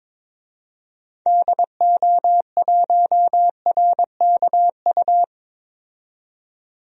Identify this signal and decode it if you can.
Morse code, heard as a keyed tone, about 22 wpm, 700 Hz. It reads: DO1RKU
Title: DXCC-100 Call Signs - 6 Characters - 22wpm